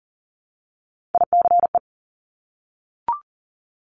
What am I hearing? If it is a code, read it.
Morse code, heard as a keyed tone, about 38 wpm, 700 Hz. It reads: ICE
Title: Multi-letter Phonemes Repeated 3x 40wpm